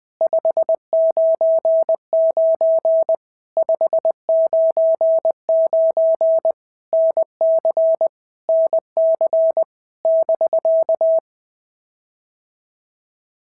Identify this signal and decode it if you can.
Morse code, heard as a keyed tone, about 20 wpm, 650 Hz. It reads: 599 599 NC NC <BK>
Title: POTA QSO Practice - 20wpm